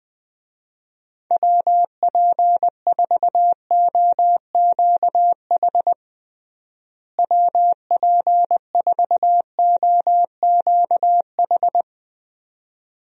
Morse code, heard as a keyed tone, about 20 wpm, 700 Hz. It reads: WP4OQH WP4OQH
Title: USA Call Signs - 6 Characters - Repeated 2x - 20wpm